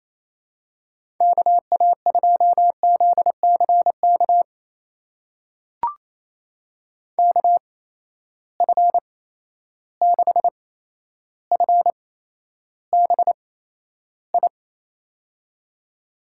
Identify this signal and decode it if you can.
Morse code, heard as a keyed tone, about 28 wpm, spaced as for 12 wpm, 700 Hz. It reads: KA2ZCK KF6FBI
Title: USA Call Signs - 6 Characters - 8x Character Spacing - 28wpm